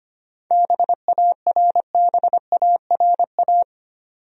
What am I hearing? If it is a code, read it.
Morse code, heard as a keyed tone, about 25 wpm, 700 Hz. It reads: BARBARA
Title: USA Common Names - 25wpm